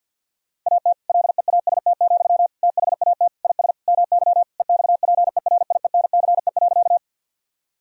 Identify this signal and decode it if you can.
Morse code, heard as a keyed tone, about 50 wpm, 700 Hz. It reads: AT LEAST, THAT IS MY EXPERIENCE.
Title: Sentences from Top 1000 Words - Review Part 3 - 50wpm